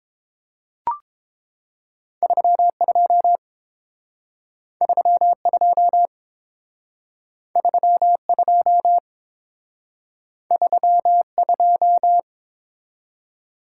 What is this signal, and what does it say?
Morse code, heard as a keyed tone, about 29 wpm, 700 Hz. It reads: 32 32 32 32